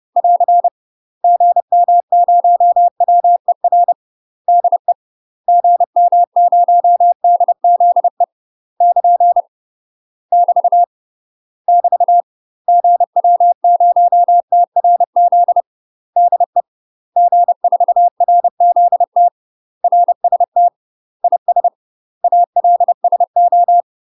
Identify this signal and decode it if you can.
Morse code, heard as a keyed tone, about 30 wpm, 700 Hz. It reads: <AR> GM0WER DE GM0DZE <KN> <BT> <BT> GW0TRZ DE G4RZT RST IS ALSO
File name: UK MASTER QSOs TRAINING/30WPM